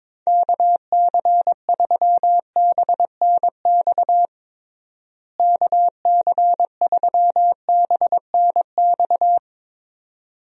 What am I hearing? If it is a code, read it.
Morse code, heard as a keyed tone, about 22 wpm, 700 Hz. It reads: KC3BNX KC3BNX
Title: USA Call Signs - 6 Characters - Repeated 2x - 22wpm